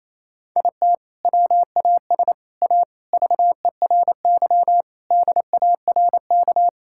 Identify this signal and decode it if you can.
Morse code, heard as a keyed tone, about 28 wpm, 700 Hz. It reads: IT WAS A VERY DARK
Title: Sentences from Top 1600 Words 28wpm